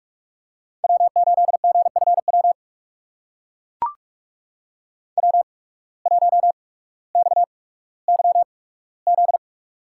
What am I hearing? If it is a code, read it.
Morse code, heard as a keyed tone, about 44 wpm, spaced as for 19 wpm, 690 Hz. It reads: W8GFW W1XYZ
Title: USA Call Signs - 5 Characters - 8x Character Spacing - 45wpm